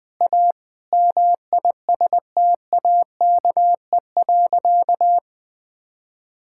Morse code, heard as a keyed tone, about 20 wpm, 700 Hz. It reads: A MISTAKE.
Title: Sentences with 5-Word Phrases - Spoken Beginning - 20wpm